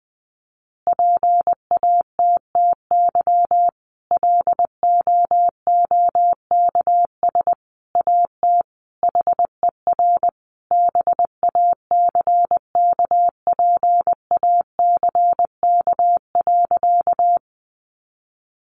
Morse code, heard as a keyed tone, about 20 wpm, 700 Hz. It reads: PATTY LOOKS AT HER BACKPACK.
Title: Additional Easy Phrases 20wpm